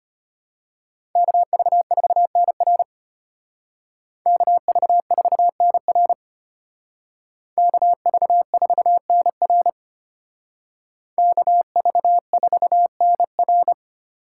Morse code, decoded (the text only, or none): KV4NR KV4NR KV4NR KV4NR